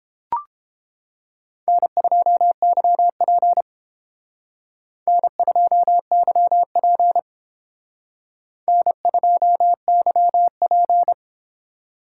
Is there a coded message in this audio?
Yes: Morse code from 1.7 to 11.1 s, heard as a keyed tone, about 28 wpm, 700 Hz. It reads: N2YP N2YP N2YP